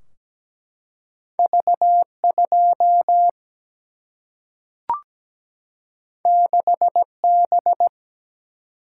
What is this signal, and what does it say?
Morse code, heard as a keyed tone, about 17 wpm, 700 Hz. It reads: V2 6B
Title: Single-Letter Single-Number 17wpm